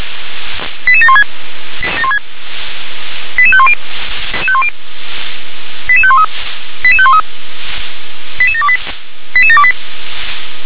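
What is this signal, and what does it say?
Five-tone selective call (ZVEI), heard as quick runs of five tones, tones of about 70 ms. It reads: 88716 88510 88513 88513 88728 88728
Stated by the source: Five-tone ZVEI selective call received on an SDR.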